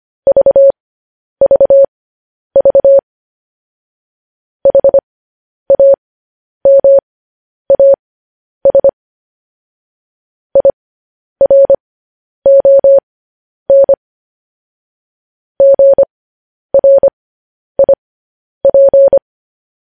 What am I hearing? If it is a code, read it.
Morse code, heard as a keyed tone, about 25 wpm, spaced as for 10 wpm, 550 Hz. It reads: VVV HAMAS IRON GRIP
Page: news headlines in morse code at 10 wpm